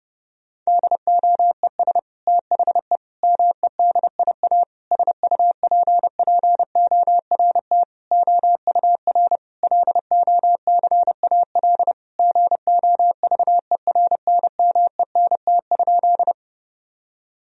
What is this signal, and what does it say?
Morse code, heard as a keyed tone, about 30 wpm, 700 Hz. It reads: DOES THE MEDIA SUPPORT OUR LOCAL GOVERNMENT?
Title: Sentences from Top 1000 Words - Review Part 4 - 30wpm